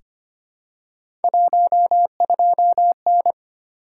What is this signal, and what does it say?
Morse code, heard as a keyed tone, about 25 wpm, 700 Hz. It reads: 12N
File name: Three Letter-Number Combinations - 25wpm